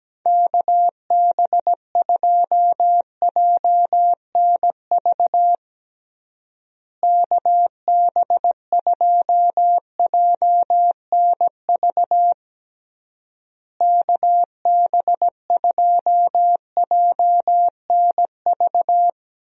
Morse code, heard as a keyed tone, about 17 wpm, 700 Hz. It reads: KB2JNV KB2JNV KB2JNV
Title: USA Call Signs - 6 Characters - Repeated 3x - 17wpm